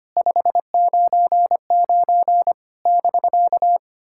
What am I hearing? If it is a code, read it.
Morse code, heard as a keyed tone, about 25 wpm, 700 Hz. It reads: 599 <BK>
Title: SOTA QSO Practice 25wpm